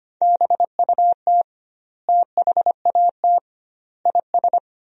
Morse code, heard as a keyed tone, about 25 wpm, 700 Hz. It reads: BUT THAT IS
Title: 2x Word-Spacing - Sentences from Top 500 Words - 25wpm